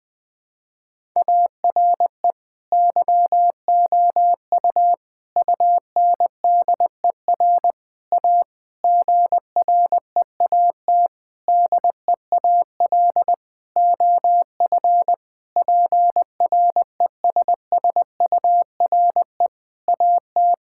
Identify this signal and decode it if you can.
Morse code, heard as a keyed tone, about 21 wpm, 700 Hz. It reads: ARE YOU UNDER A GREAT DEAL OF PRESSURE AT
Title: Sentences from Top 1000 Words - Review Part 6 - 20wpm